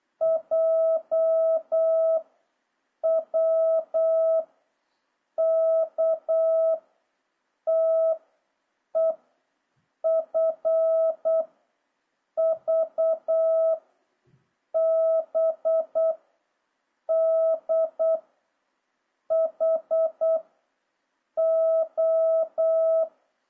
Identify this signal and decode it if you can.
Morse code, heard as a keyed tone, about 8 wpm, 650 Hz. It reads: J W K T E F V B D H O
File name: jh8cbh / 2026cw